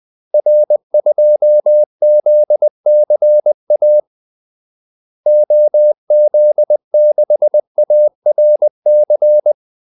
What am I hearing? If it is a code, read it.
Morse code, heard as a keyed tone, about 20 wpm, 600 Hz. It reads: R2ZCA OZ6ARC
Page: Kaldesignaler 20 wpm